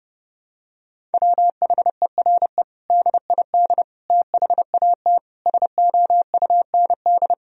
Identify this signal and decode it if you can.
Morse code, heard as a keyed tone, about 30 wpm, 700 Hz. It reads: WHERE DID THAT SOUND
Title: Sentences from Top 1000 Words - Review Part 1 - 30wpm